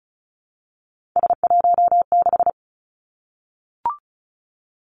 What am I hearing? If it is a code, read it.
Morse code, heard as a keyed tone, about 36 wpm, 700 Hz. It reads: S16
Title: Three Challenging Letter-Number Combinations 35wpm